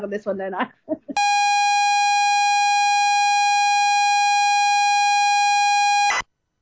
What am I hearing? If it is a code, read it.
ZX Spectrum tape loading, with speech.